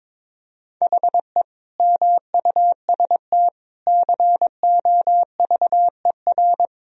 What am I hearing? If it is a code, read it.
Morse code, heard as a keyed tone, about 22 wpm, 700 Hz. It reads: HE MUST COVER
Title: Sentences from Top 500 Words and Abbreviations 22wpm